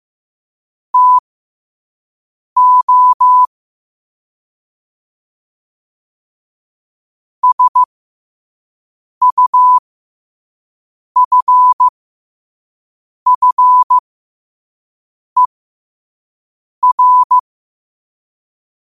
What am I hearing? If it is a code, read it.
Morse code, heard as a keyed tone, about 15 wpm, spaced as for 5 wpm, 1000 Hz. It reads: TO SUFFER
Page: New quotes every day in morse code at 5 Words per minute.